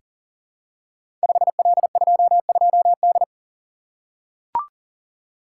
Morse code, heard as a keyed tone, about 42 wpm, 700 Hz. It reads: 5L22D
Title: Code Groups - 5 Letter:Numbers - 40wpm